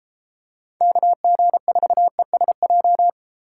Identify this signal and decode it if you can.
Morse code, heard as a keyed tone, about 33 wpm, 700 Hz. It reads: KG4ESJ